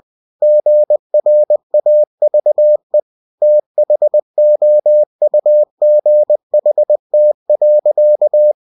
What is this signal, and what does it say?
Morse code, heard as a keyed tone, about 20 wpm, 600 Hz. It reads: GRAVE THOUGHT.
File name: cw-books / winnie-the-pooh / winnie-20wpm-page-0098